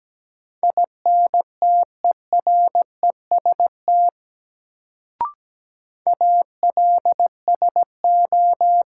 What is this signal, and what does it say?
Morse code, heard as a keyed tone, about 17 wpm, 700 Hz. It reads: INTEREST ALSO
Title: Sentences from Top 400 Words - New Word Drill - 17wpm